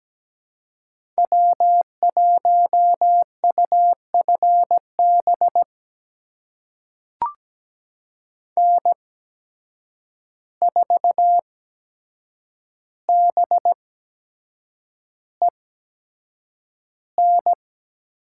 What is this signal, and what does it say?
Morse code, heard as a keyed tone, about 17 wpm, spaced as for 7 wpm, 700 Hz. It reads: W1UFB N4BEN
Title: USA Call Signs - 5 Characters - 8x Character Spacing - 17wpm